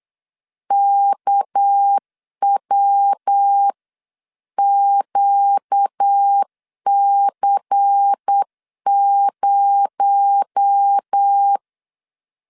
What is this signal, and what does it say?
Morse code, heard as a keyed tone, about 8 wpm, 800 Hz. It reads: KWQC0
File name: CW / DET / 1 9 1500C S C 8wpm 5653